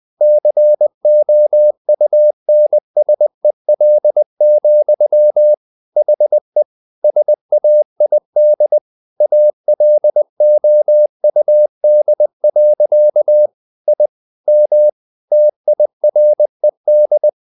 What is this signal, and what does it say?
Morse code, heard as a keyed tone, about 20 wpm, 600 Hz. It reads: COUNSEL, HE SAID ALOUD. I M TIRED
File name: the old man and the sea part 2 20wpm